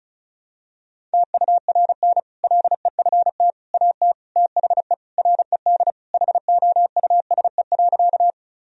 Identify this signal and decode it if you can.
Morse code, heard as a keyed tone, about 36 wpm, 700 Hz. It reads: TURN LEFT AT THE RED HOUSE.
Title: Sentences from Top 1000 Words - Review Part 2 - 35wpm